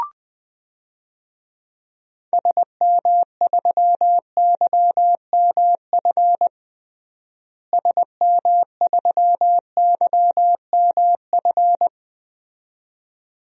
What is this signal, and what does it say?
Morse code, heard as a keyed tone, about 21 wpm, 700 Hz. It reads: SM3YMF SM3YMF
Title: DXCC-100 Call Signs - 6 Characters - Repeated 2x - 20wpm